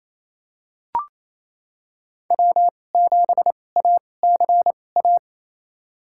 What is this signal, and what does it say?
Morse code, heard as a keyed tone, about 28 wpm, spaced as for 20 wpm, 700 Hz. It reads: W7ACA
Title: USA Call Signs - 5 Characters - 2x Character Spacing - 28wpm